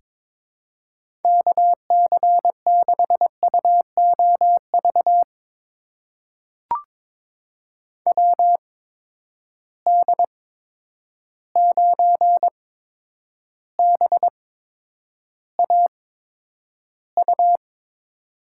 Morse code, heard as a keyed tone, about 22 wpm, spaced as for 9 wpm, 700 Hz. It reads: KC6UOV WD9BAU
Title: USA Call Signs - 6 Characters - 8x Character Spacing - 22wpm